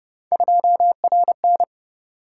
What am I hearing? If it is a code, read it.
Morse code, heard as a keyed tone, about 30 wpm, 700 Hz. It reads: 2RN